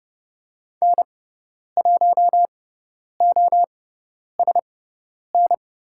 Morse code, heard as a keyed tone, about 30 wpm, spaced as for 10 wpm, 700 Hz. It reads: N1OSN
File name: Lesson 18g - Letter m - Call Signs 30wpm